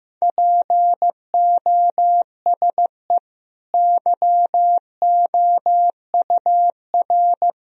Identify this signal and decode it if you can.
Morse code, heard as a keyed tone, about 15 wpm, 700 Hz. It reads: POSE YOUR
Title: Sentences from Top 1000 Words - Review Part 2 - 15wpm